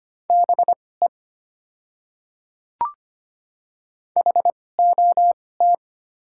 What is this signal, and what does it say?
Morse code, heard as a keyed tone, about 25 wpm, spaced as for 18 wpm, 700 Hz. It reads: BE HOT
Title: 2x Character Spacing - Top 100 Words - 25wpm